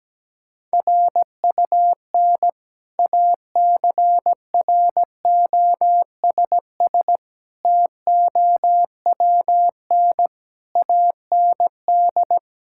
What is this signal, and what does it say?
Morse code, heard as a keyed tone, about 17 wpm, 700 Hz. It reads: RUN ACROSS TOWN AND
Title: Sentences from Top 1000 Words - Review Part 5 - 17wpm